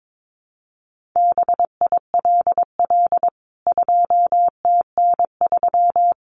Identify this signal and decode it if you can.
Morse code, heard as a keyed tone, about 22 wpm, 700 Hz. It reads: BILL 2TN3
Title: CWT Exchange 22wpm